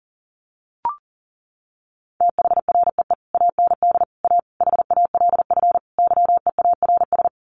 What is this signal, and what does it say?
Morse code, heard as a keyed tone, about 40 wpm, 700 Hz. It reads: THREE AND A HALF YEARS
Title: Sets of 5 Words Encore2 40wpm